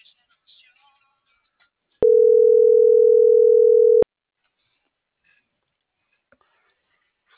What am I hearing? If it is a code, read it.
Ringback tone (North America), 440 + 480 Hz.